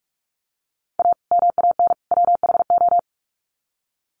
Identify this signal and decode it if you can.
Morse code, heard as a keyed tone, about 44 wpm, 700 Hz. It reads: A MAN WHO
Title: Sets of 3 Words 45wpm